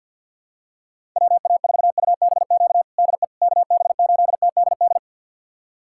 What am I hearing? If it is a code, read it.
Morse code, heard as a keyed tone, about 49 wpm, 700 Hz. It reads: WA4UBQ DE KB8TDD
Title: USA Callsign DE Callsign - 6 Characters - 50wpm